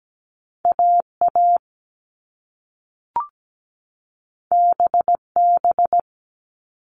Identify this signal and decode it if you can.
Morse code, heard as a keyed tone, about 17 wpm, 700 Hz. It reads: AA BB
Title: All Cut Numbers - 2-Digit 17wpm